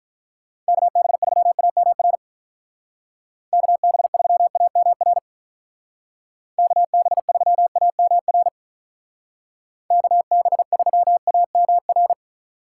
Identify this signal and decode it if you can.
Morse code, heard as a keyed tone, about 42 wpm, 700 Hz. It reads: KB3AMR KB3AMR KB3AMR KB3AMR
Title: Speed Racing - USA 2x3 Call Signs 35wpm